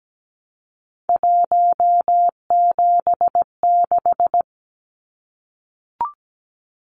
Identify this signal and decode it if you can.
Morse code, heard as a keyed tone, about 17 wpm, 700 Hz. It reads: 176
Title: Numbers - 3-Digit 17wpm